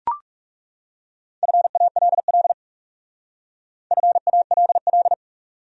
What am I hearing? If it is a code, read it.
Morse code, heard as a keyed tone, about 40 wpm, 700 Hz. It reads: FALL FALL